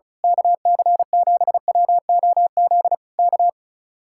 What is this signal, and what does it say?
Morse code, heard as a keyed tone, about 34 wpm, 700 Hz. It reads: KC7WOZ K
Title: Speed-Racing Common QSO Phrases - 30wpm